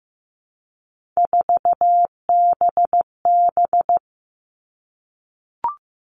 Morse code, heard as a keyed tone, about 15 wpm, 700 Hz. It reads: 4BB
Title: All Cut Numbers - 3-Digit 15wpm